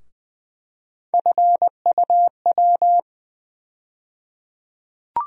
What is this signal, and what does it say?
Morse code, heard as a keyed tone, about 21 wpm, 700 Hz. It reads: FUW
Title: USA Call Sign Postfixes - 3 Letters 20wpm